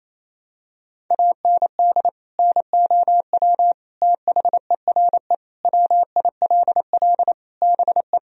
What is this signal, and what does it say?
Morse code, heard as a keyed tone, about 28 wpm, 700 Hz. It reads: AND NOW THERE WILL BE
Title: Sentences from Top 1000 Words - Review Part 2 - 28wpm